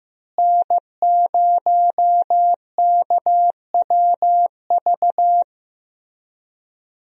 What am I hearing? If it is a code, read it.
Morse code, heard as a keyed tone, about 15 wpm, 700 Hz. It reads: N0KWV
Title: USA Call Signs - 5 Characters - Repeated 3x - 15wpm